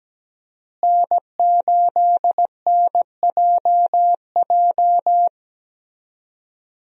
Morse code, heard as a keyed tone, about 17 wpm, 700 Hz. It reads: N8NJJ
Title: USA Call Signs - 5 Characters - Repeated 2x - 17wpm